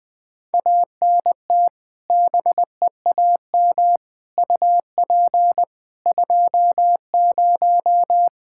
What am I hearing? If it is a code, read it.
Morse code, heard as a keyed tone, about 20 wpm, 700 Hz. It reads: ANT BEAM UP 20
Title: Practice QSO - Exchange 2 - Info - 20wpm